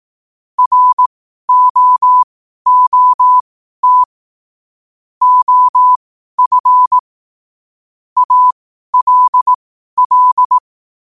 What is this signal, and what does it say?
Morse code, heard as a keyed tone, about 18 wpm, spaced as for 12 wpm, 1000 Hz. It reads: ROOT OF ALL
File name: qotdfiles / 12wpm